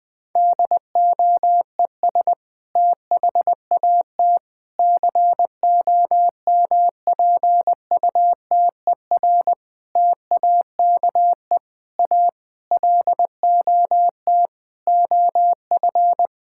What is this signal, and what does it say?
Morse code, heard as a keyed tone, about 20 wpm, 700 Hz. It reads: DOES THAT COMPUTER TAKE A LOT OF
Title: Sentences from Top 1000 Words - Review Part 6 - 20wpm